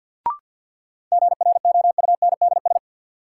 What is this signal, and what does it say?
Morse code, heard as a keyed tone, about 49 wpm, 700 Hz. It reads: GROUNDS
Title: Sentences from Top 1300 Words - New Word Drill Alt - 50wpm